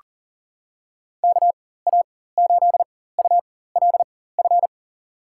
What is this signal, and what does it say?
Morse code, heard as a keyed tone, about 37 wpm, spaced as for 18 wpm, 700 Hz. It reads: KA8ULF
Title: USA Call Signs - 4x Character Spacing - 40wpm